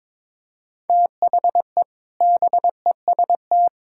Morse code, heard as a keyed tone, about 22 wpm, 700 Hz. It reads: THE BEST
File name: practice / 22wpm / Sets of 6 Words - 22wpm